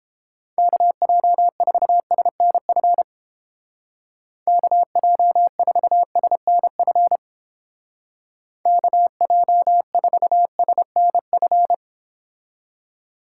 Morse code, heard as a keyed tone, about 30 wpm, 700 Hz. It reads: KJ4SNF KJ4SNF KJ4SNF